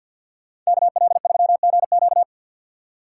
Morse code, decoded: KL3GQ

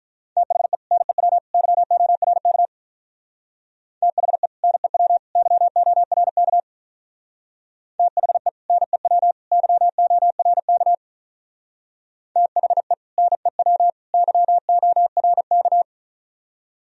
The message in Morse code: THE NEW YORK THE NEW YORK THE NEW YORK THE NEW YORK

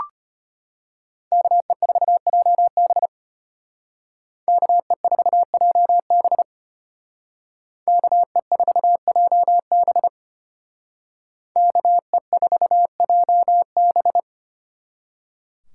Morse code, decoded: KE4JB KE4JB KE4JB KE4JB

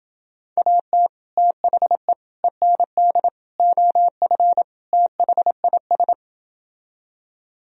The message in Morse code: AT THE END OF THIS

